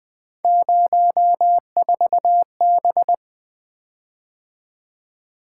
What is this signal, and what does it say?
Morse code, heard as a keyed tone, about 20 wpm, 700 Hz. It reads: 04B
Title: Three Letter-Number Combinations - 20wpm